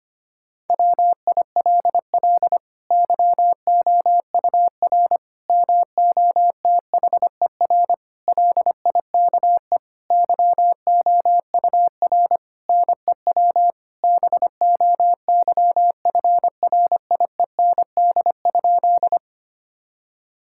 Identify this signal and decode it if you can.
Morse code, heard as a keyed tone, about 25 wpm, 700 Hz. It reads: WILL YOUR MOTHER LIKE YOUR NEW BOYFRIEND?
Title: Sentences from Top 1000 Words - Review Part 5 - 25wpm